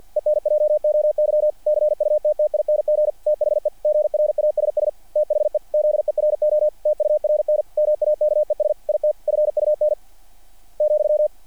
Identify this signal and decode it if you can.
Morse code, heard as a keyed tone, about 50 wpm, 600 Hz. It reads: ENJOY CUTTING THE GRASS THE ZERO TURN MAKES IT FUN ,
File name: archived / CRSnet